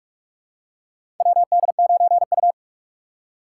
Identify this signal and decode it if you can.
Morse code, heard as a keyed tone, about 44 wpm, 700 Hz. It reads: WD9U
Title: USA Call Signs - 4 Characters - 6x Character Spacing - 45wpm